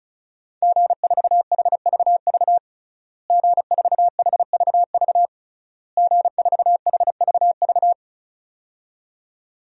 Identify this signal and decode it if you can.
Morse code, heard as a keyed tone, about 34 wpm, 700 Hz. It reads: G4HVV G4HVV G4HVV
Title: DXCC-100 Call Signs - 5 Characters - Repeated 3x - 35wpm